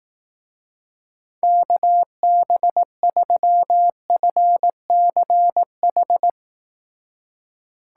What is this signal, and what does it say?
Morse code, heard as a keyed tone, about 18 wpm, 700 Hz. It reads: KB3FCH
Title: Speed Racing - USA 2x3 Call Signs 15wpm